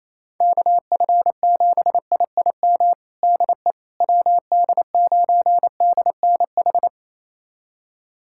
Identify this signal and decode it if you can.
Morse code, heard as a keyed tone, about 27 wpm, 700 Hz. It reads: KF7IIM DE WD9DNH
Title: USA Callsign DE Callsign - 6 Characters - 28wpm